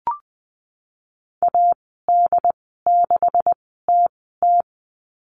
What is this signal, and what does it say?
Morse code, heard as a keyed tone, about 20 wpm, spaced as for 15 wpm, 700 Hz. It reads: AD6TT
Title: 193 Lesson 31g - Letter v - Call Signs 20wpm